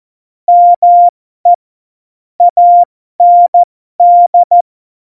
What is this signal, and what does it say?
Morse code, heard as a keyed tone, about 14 wpm, spaced as for 12 wpm, 700 Hz. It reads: ME AND
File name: Morsekurs / Witze